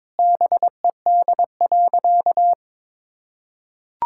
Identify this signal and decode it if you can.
Morse code, heard as a keyed tone, about 22 wpm, 700 Hz. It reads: BED.
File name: practice / 22wpm / Sentences with 3-Word Phrases - 22wpm